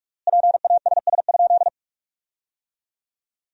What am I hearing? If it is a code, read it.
Morse code, heard as a keyed tone, about 46 wpm, 700 Hz. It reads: PASS?
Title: 4x Word-Spacing - Sentences from Top 400 Words - 45wpm